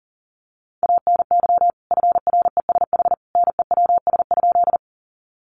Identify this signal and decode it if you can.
Morse code, heard as a keyed tone, about 40 wpm, 700 Hz. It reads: ANY FRESH NEWS?
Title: Sentences from Top 1200 Words 40wpm